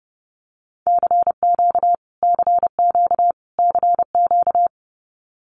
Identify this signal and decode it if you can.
Morse code, heard as a keyed tone, about 30 wpm, 700 Hz. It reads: CQ CQ CQ
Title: Abbreviations Repeated 3x - 30wpm